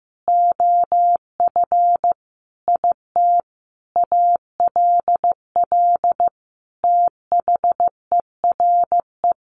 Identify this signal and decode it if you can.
Morse code, heard as a keyed tone, about 14 wpm, 700 Hz. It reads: OF IT ALL THERE
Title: Sentences from Top 1100 Words 15wpm